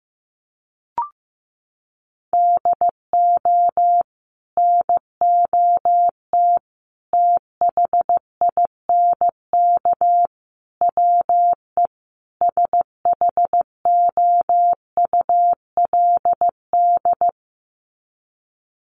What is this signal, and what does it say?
Morse code, heard as a keyed tone, about 15 wpm, 700 Hz. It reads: DO NOT THINK WE SHOULD
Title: Sets of 5 Words Encore 15wpm